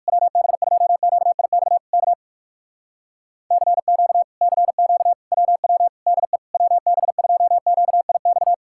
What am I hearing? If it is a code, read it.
Morse code, heard as a keyed tone, about 66 wpm, spaced as for 56 wpm, 700 Hz. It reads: WB2QIX K CQ CQ WW DE WB2QIX